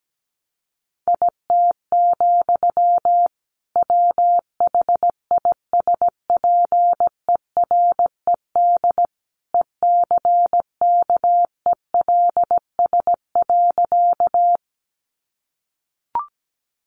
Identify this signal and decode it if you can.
Morse code, heard as a keyed tone, about 17 wpm, 700 Hz. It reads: IT, WHISPERED ECKELS.
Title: A Sound Of Thunder 17wpm